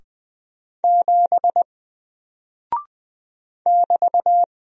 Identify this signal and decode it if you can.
Morse code, heard as a keyed tone, about 20 wpm, 700 Hz. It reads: 7 <BT>
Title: Single Character (Letter:Number:Punctuation:Prosign) 20wpm